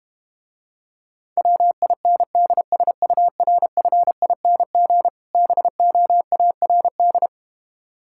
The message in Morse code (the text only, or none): WINDSURFING BOARD